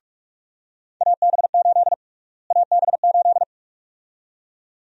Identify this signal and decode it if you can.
Morse code, heard as a keyed tone, about 46 wpm, 700 Hz. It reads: AB8 AB8
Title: USA Call Sign Prefixes - Repeated 2x - 45wpm